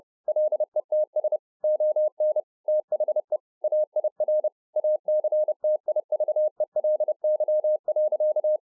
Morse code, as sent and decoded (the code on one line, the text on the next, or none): .-.. - ... / --- -. / - .... / .- .. .-. / .- -.-. - .. ...- . .-.. -.-- .-.-.-
LETS ON THE AIR ACTIVELY.